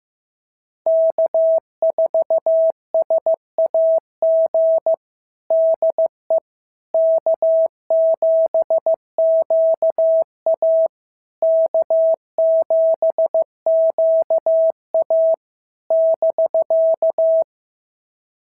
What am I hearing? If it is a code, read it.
Morse code, heard as a keyed tone, about 14 wpm, 650 Hz. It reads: K4SAG DE K7QA K7QA <BK>
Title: SOTA QSO Practice 15wpm